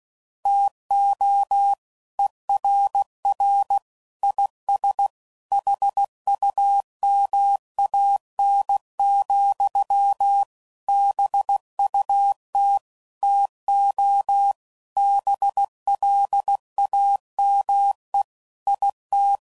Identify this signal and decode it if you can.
Morse code, heard as a keyed tone, about 16 wpm, 800 Hz. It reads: TO ERR IS HUMAN, BUT TO BLAME IT